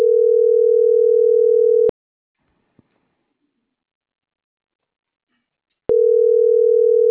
Ringback tone (North America), 440 + 480 Hz.